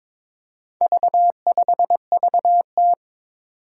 Morse code, heard as a keyed tone, about 22 wpm, 700 Hz. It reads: V5VT
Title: Four Challenging Letter-Number Combinations 22wpm